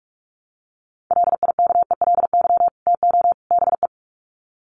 Morse code, heard as a keyed tone, about 44 wpm, 700 Hz. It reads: LIKELY TO BE
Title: Sets of 3 Words 45wpm